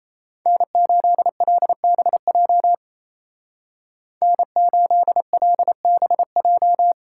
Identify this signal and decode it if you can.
Morse code, heard as a keyed tone, about 33 wpm, 700 Hz. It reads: N8LBJ N8LBJ
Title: Speed Racing - USA 1x3 Call Signs 28wpm